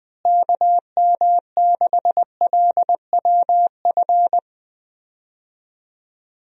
Morse code, heard as a keyed tone, about 21 wpm, 700 Hz. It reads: KM6LWF
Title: USA Call Signs - 6 Characters - Repeated 2x - 20wpm